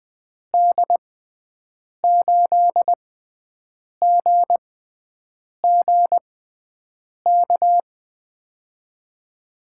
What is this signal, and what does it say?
Morse code, heard as a keyed tone, about 20 wpm, spaced as for 7 wpm, 700 Hz. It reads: D8GGK
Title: USA Call Signs - 6 Characters - 6x Character Spacing - 20wpm